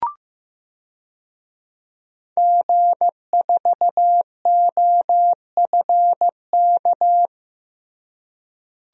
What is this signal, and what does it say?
Morse code, heard as a keyed tone, about 15 wpm, 700 Hz. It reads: G4OFK
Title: DXCC-100 Call Signs - 5 Characters - Repeated 3x - 15wpm